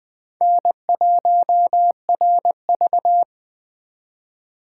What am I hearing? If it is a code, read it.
Morse code, heard as a keyed tone, about 20 wpm, 700 Hz. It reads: N1RV
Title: Speed Racing - USA 1x2 Call Signs 17wpm